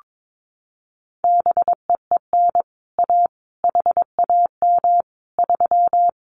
Morse code, heard as a keyed tone, about 22 wpm, 700 Hz. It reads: BEEN A HAM 3
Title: Common QSO Phrases Encore 22wpm